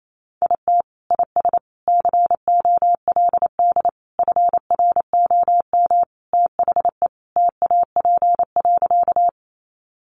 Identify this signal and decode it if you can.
Morse code, heard as a keyed tone, about 28 wpm, 700 Hz. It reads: IT IS COLD FROM THE TAP.
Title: Sentences from Top 1400 Words 28wpm